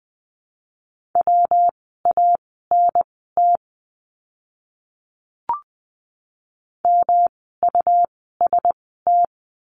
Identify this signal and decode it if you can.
Morse code, heard as a keyed tone, about 20 wpm, spaced as for 14 wpm, 700 Hz. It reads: WANT MUST
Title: 237 Lesson 38f - Letter q - Words 20wpm